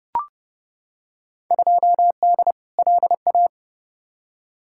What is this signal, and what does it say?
Morse code, heard as a keyed tone, about 30 wpm, 700 Hz. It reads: 2D LA
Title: Field Day Exchanges - 30wpm